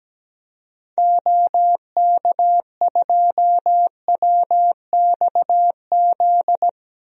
Morse code, heard as a keyed tone, about 17 wpm, 700 Hz. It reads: OK2WXZ
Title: DXCC-100 Call Signs - 6 Characters - 17wpm